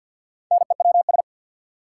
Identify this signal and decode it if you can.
Morse code, heard as a keyed tone, about 52 wpm, 700 Hz. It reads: NEWS
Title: Sentences from Top 800 Words - New Word Drill - 50wpm